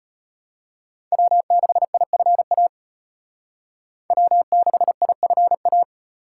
Morse code, heard as a keyed tone, about 29 wpm, 700 Hz. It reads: W6IFA W6IFA
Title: Speed Racing - USA 1x3 Call Signs 25wpm